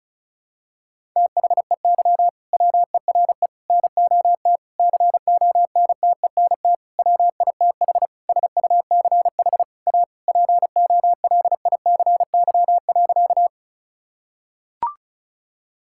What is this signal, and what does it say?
Morse code, heard as a keyed tone, about 36 wpm, 700 Hz. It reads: THEY WERE NOT CONTENT WITH SUCH A POLICY.
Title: Sentences from Top 1000 Words - Review Part 6 - 35wpm